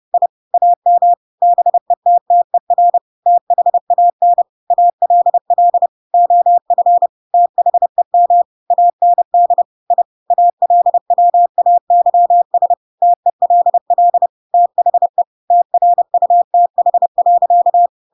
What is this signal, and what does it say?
Morse code, heard as a keyed tone, about 30 wpm, 700 Hz. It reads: I AM BETTER THAN ALL OF THEM AND I ALWAYS TELL THE TRUTH.
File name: Pinocchio / Pinocchio ch8 30 30 ik1hge